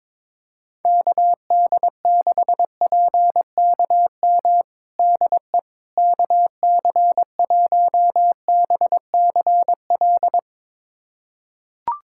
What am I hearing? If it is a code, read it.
Morse code, heard as a keyed tone, about 21 wpm, 700 Hz. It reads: KD6PKM DE KC1BCL
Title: USA Callsign DE Callsign 22wpm